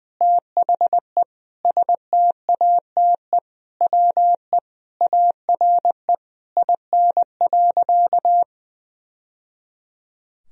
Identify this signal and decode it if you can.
Morse code, heard as a keyed tone, about 20 wpm, 700 Hz. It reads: THE STATE WE ARE IN.
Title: Sentences from Top 1000 Words - Review Part 2 - 20wpm